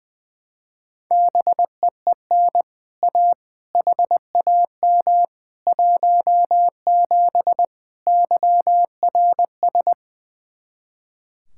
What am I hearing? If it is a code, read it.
Morse code, heard as a keyed tone, about 20 wpm, 700 Hz. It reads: BEEN A HAM 17 YRS
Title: Speed-Racing Common QSO Phrases - 20wpm